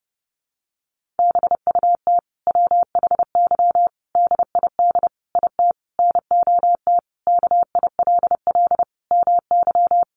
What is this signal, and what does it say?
Morse code, heard as a keyed tone, about 30 wpm, 700 Hz. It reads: BUT WHY DID IT NOT KILL MY
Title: Sentences from Top 1400 Words 30wpm